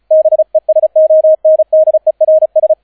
Morse code, heard as a keyed tone, about 36 wpm, 630 Hz, with radio noise. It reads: BESONDERS